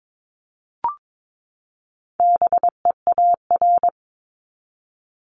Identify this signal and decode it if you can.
Morse code, heard as a keyed tone, about 22 wpm, 700 Hz. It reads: BEAR
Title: ING Suffix - Top 100 Words with Base Words 22wpm